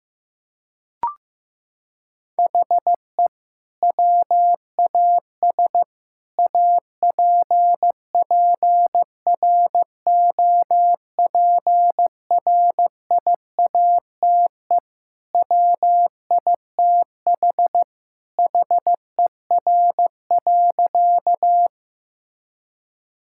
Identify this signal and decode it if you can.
Morse code, heard as a keyed tone, about 15 wpm, 700 Hz. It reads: HE WAS APPROPRIATE WITH HER.
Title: Sentences from Top 1000 Words - Review Part 5 - 15wpm